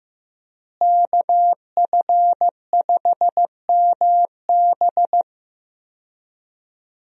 Morse code, heard as a keyed tone, about 15 wpm, 700 Hz. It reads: KF5MB
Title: USA Call Signs - 5 Characters - Repeated 2x - 15wpm